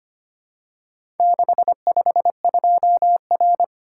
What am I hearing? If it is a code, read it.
Morse code, heard as a keyed tone, about 25 wpm, 700 Hz. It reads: 652R
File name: practice / 25wpm / Four Letter-Number Combinations - 25wpm